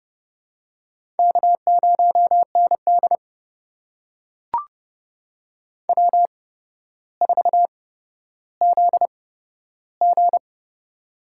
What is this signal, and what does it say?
Morse code, heard as a keyed tone, about 32 wpm, spaced as for 13 wpm, 700 Hz. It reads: K0ND W4ZG
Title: USA Call Signs - 4 Characters - 8x Character Spacing - 30wpm